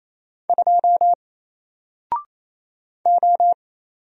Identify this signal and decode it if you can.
Morse code, heard as a keyed tone, about 27 wpm, 700 Hz. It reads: 2 O
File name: Single-Letter-Number-28wpm